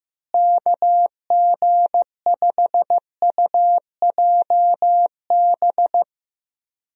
Morse code, heard as a keyed tone, about 15 wpm, 700 Hz. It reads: KG5UJB
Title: USA Call Signs - 6 Characters - Repeated 3x - 15wpm